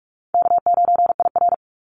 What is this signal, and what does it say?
Morse code, heard as a keyed tone, about 44 wpm, 700 Hz. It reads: K9IR
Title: NA SOTA Activator Callsigns 45wpm